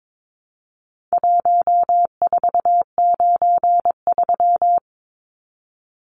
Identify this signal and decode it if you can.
Morse code, heard as a keyed tone, about 22 wpm, 700 Hz. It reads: 1493
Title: Numbers - Contest Style - 4-Digit 22wpm